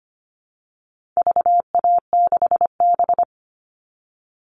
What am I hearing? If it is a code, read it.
Morse code, heard as a keyed tone, about 25 wpm, 700 Hz. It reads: VA6B
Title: All Cut Numbers - 4-Digit 25wpm